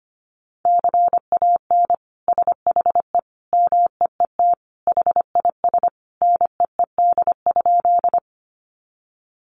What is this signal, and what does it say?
Morse code, heard as a keyed tone, about 25 wpm, 700 Hz. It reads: CAN SHE MEET HIS NEED?
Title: Sentences from Top 1100 Words 25wpm